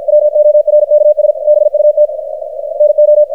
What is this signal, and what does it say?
Morse code, heard as a keyed tone, about 52 wpm, 600 Hz, with radio noise. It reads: WORKS BUT NO
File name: archived / CRSnet